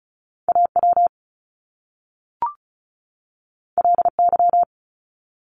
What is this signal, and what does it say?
Morse code, heard as a keyed tone, about 36 wpm, 700 Hz. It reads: A W L Y
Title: Top Two Letter Combinations in Words 35wpm